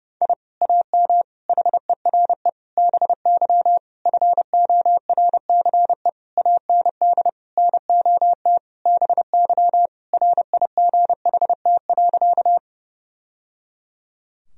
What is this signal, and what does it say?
Morse code, heard as a keyed tone, about 32 wpm, 700 Hz. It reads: I AM HERE BY FORCE AND NOT BY RIGHT.
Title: Sentences from Top 1000 Words - Review Part 3 - 30wpm